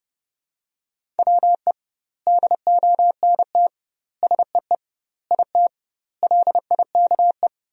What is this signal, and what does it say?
Morse code, heard as a keyed tone, about 32 wpm, 700 Hz. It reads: WE DONT SEE IT LIKE
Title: 2x Word-Spacing - Sentences from Top 300 Words - 30wpm